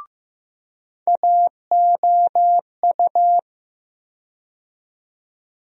Morse code, heard as a keyed tone, about 15 wpm, 700 Hz. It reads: AOU
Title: USA Call Sign Postfixes - 3 Letters 15wpm